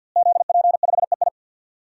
Morse code, heard as a keyed tone, about 48 wpm, 700 Hz. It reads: ZP5EI
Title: DXCC-100 Call Signs - 5 Characters - Repeated 2x - 50wpm